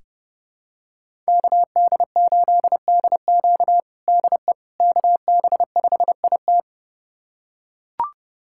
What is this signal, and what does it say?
Morse code, heard as a keyed tone, about 30 wpm, 700 Hz. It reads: KD8DQ DE KB5IT
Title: USA Callsign DE Callsign - 5 Characters - 30wpm